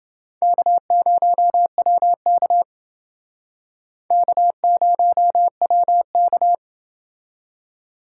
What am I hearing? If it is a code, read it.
Morse code, heard as a keyed tone, about 28 wpm, 700 Hz. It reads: K0WK K0WK